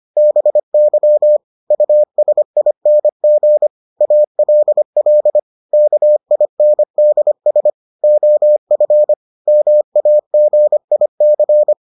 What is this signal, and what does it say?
Morse code, heard as a keyed tone, about 25 wpm, 600 Hz. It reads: BY USING ALL KINDS OF MAGIC